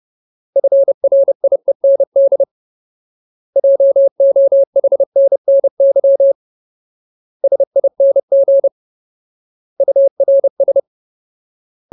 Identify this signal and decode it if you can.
Morse code, heard as a keyed tone, about 30 wpm, 550 Hz. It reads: FRIEND JOHNNY SING URS
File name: Ultimate Word List / 30WPM Ultimate